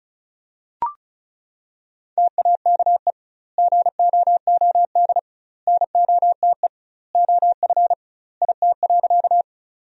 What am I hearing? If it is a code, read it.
Morse code, heard as a keyed tone, about 34 wpm, 700 Hz. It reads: TAKE GOOD NOTE OF IT.
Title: 2x Word-Spacing - Sentences from Top 500 Words - 35wpm